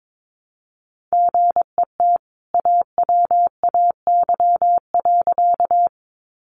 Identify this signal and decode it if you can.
Morse code, heard as a keyed tone, about 22 wpm, 700 Hz. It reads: GET AWAY.
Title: Sentences from Top 600 Words 22wpm